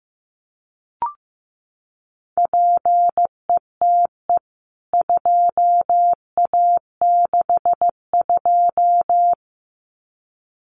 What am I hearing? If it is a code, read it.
Morse code, heard as a keyed tone, about 15 wpm, 700 Hz. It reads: PETE 2A62
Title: CWT Exchange 15wpm